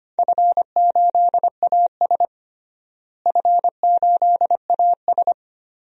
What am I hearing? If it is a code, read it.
Morse code, heard as a keyed tone, about 25 wpm, 700 Hz. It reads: F8AS F8AS
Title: DXCC-100 Call Signs - 4 Characters - Repeated 3x - 25wpm